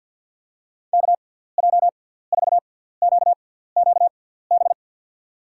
Morse code, heard as a keyed tone, about 53 wpm, spaced as for 17 wpm, 700 Hz. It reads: KJ4QQB
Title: USA Call Signs - 6x Character Spacing - 50wpm